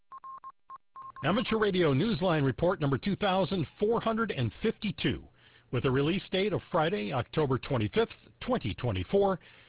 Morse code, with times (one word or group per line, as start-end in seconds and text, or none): none